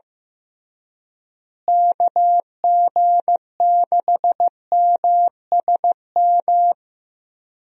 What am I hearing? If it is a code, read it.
Morse code, heard as a keyed tone, about 16 wpm, 700 Hz. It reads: KG6MSM
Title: USA Call Signs - 6 Characters - Repeated 3x - 15wpm